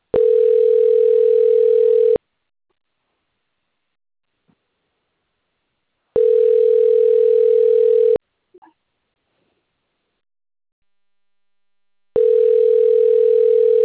Ringback tone (North America), 440 + 480 Hz.